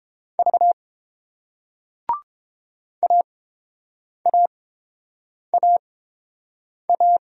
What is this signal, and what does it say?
Morse code, heard as a keyed tone, about 30 wpm, spaced as for 7 wpm, 690 Hz. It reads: V AAAA